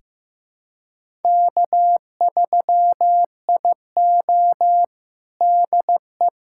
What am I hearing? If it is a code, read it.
Morse code, heard as a keyed tone, about 16 wpm, 700 Hz. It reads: K3IO DE
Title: USA Callsign DE Callsign - 4 Characters - 15wpm